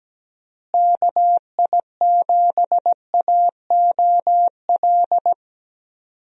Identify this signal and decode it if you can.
Morse code, heard as a keyed tone, about 17 wpm, 700 Hz. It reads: KI7AOL
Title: USA Call Signs - 6 Characters - Repeated 3x - 17wpm